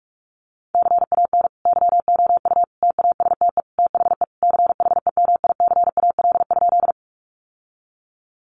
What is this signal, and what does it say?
Morse code, heard as a keyed tone, about 44 wpm, 700 Hz. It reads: CAN YOU TASTE THE CHEMICAL?
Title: Sentences from Top 1200 Words 45wpm